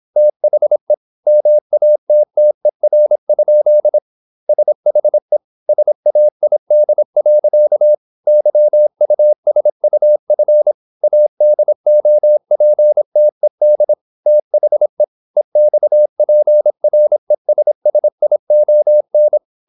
Morse code, heard as a keyed tone, about 26 wpm, 600 Hz. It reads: THE MATTER? SHE SAID. YUSUF ADOPTED THE EXPRESSION
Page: Never 26wpm | CW med Gnister